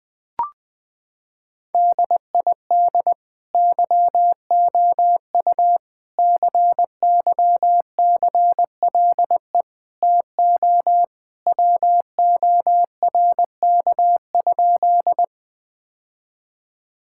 Morse code, heard as a keyed tone, about 21 wpm, 700 Hz. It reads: DID YOU CYCLE TO WORK?
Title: Sentences from Top 1000 Words - Review Part 6 - 20wpm